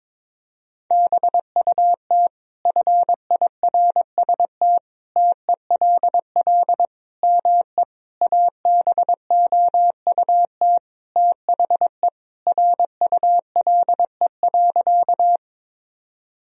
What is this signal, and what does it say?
Morse code, heard as a keyed tone, about 22 wpm, 700 Hz. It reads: BUT FIRST TELL ME ABOUT THE RULE.
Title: Sentences from Top 1000 Words - Review Part 2 - 22wpm